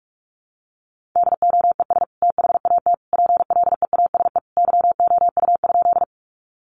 Morse code, heard as a keyed tone, about 44 wpm, 700 Hz. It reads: DOES THAT PLEASE YOU?
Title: Sentences from Top 1500 Words 45wpm